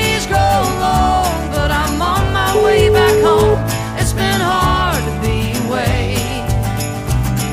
Ringback tone (Europe, 425 Hz), with noise.